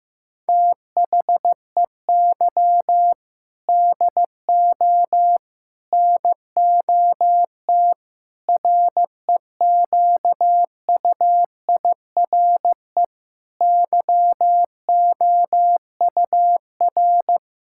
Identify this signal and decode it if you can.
Morse code, heard as a keyed tone, about 15 wpm, 700 Hz. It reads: THEY DO NOT REQUIRE YOUR
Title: Sentences from Top 1000 Words - Review Part 6 - 15wpm